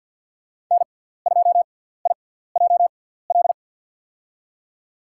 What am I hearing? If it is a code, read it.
Morse code, heard as a keyed tone, about 52 wpm, spaced as for 17 wpm, 700 Hz. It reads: N2IJL
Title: USA Call Signs - 5 Characters - 6x Character Spacing - 50wpm